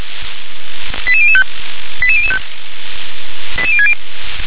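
Five-tone selective call (ZVEI), heard as quick runs of five tones, tones of about 70 ms. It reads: 88005 88005 88060
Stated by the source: ZVEI five-tone selective call received by SDR.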